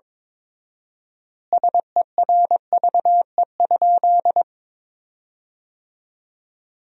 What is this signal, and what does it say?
Morse code, heard as a keyed tone, about 22 wpm, 700 Hz. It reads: SERVE?
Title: 4x Word-Spacing - Sentences from Top 400 Words - 22wpm